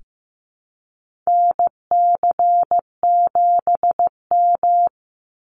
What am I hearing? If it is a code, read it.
Morse code, heard as a keyed tone, about 14 wpm, 700 Hz. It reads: NC7M
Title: Call Signs Repeated 3x - 15wpm